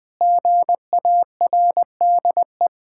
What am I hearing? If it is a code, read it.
Morse code, heard as a keyed tone, about 20 wpm, 700 Hz. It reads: GARDE
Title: Sentences with 3-Word Phrases - Spoken Beginning - 20wpm